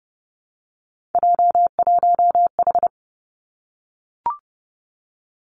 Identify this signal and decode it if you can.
Morse code, heard as a keyed tone, about 30 wpm, 700 Hz. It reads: J1H
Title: Three Challenging Letter-Number Combinations 30wpm